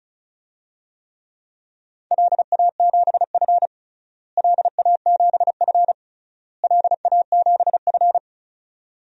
Morse code, heard as a keyed tone, about 36 wpm, 700 Hz. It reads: LA7F LA7F LA7F
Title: DXCC-100 Call Signs - 4 Characters - Repeated 3x - 35wpm